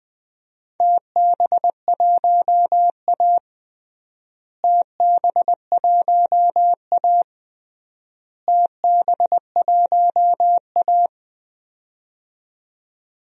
Morse code, heard as a keyed tone, about 21 wpm, 700 Hz. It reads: TB1A TB1A TB1A
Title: DXCC-100 Call Signs - 4 Characters - Repeated 3x - 20wpm